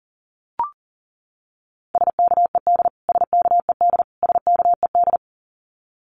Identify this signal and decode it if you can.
Morse code, heard as a keyed tone, about 40 wpm, 700 Hz. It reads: SKED SKED SKED
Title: Abbreviations Repeated 3x - 40wpm